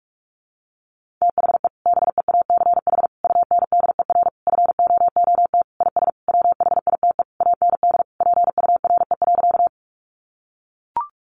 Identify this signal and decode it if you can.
Morse code, heard as a keyed tone, about 44 wpm, 700 Hz. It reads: THE BEACH UNDER FOOT IS WHITE AND PURE.
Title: Sentences from Top 1500 Words 45wpm